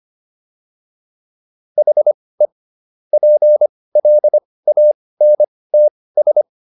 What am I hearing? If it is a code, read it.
Morse code, heard as a keyed tone, about 25 wpm, spaced as for 18 wpm, 600 Hz. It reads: HE PLANTS